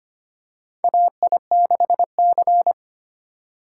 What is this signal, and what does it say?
Morse code, heard as a keyed tone, about 25 wpm, 700 Hz. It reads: AI6C
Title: USA Call Signs - 4 Characters - 3x Character Spacing - 25wpm